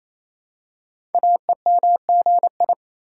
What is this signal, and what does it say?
Morse code, heard as a keyed tone, about 27 wpm, 700 Hz. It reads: AEMGI
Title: Code Groups - 5 Letters - 28wpm